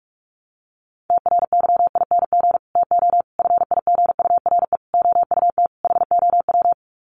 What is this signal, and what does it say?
Morse code, heard as a keyed tone, about 44 wpm, 700 Hz. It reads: TRYING TO FIGURE OUT HOW
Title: Sets of 5 Words Encore 45wpm